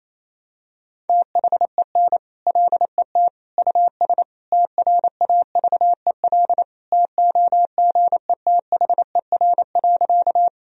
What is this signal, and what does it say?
Morse code, heard as a keyed tone, about 28 wpm, 700 Hz. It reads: THEN LET US TRAVEL TOGETHER.
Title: Sentences from Top 1000 Words - Review Part 2 - 28wpm